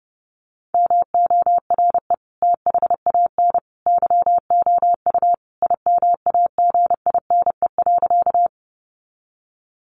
Morse code, heard as a keyed tone, about 30 wpm, 700 Hz. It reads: MORE THAN YOU IMAGINE.
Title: Sentences from Top 1400 Words 30wpm